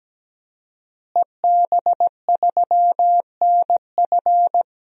Morse code, heard as a keyed tone, about 17 wpm, 700 Hz. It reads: EB3NF
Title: DXCC-100 Call Signs - 5 Characters - Repeated 3x - 17wpm